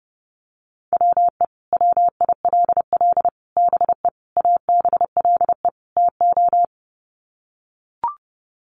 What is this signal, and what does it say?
Morse code, heard as a keyed tone, about 30 wpm, 700 Hz. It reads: WE WILL BE ABLE TO
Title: Sets of 5 Words Encore 30wpm